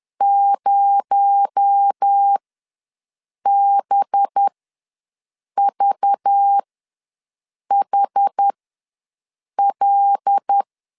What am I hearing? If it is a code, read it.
Morse code, heard as a keyed tone, about 10 wpm, spaced as for 6 wpm, 800 Hz. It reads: 0BVHL
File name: LEZIONI / 6 PKX0 / 6 PKX0 ws 10wpm 4220